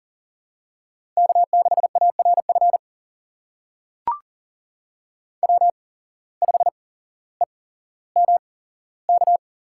Morse code, heard as a keyed tone, about 42 wpm, spaced as for 17 wpm, 700 Hz. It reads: K6ARF W5EMK
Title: USA Call Signs - 5 Characters - 8x Character Spacing - 40wpm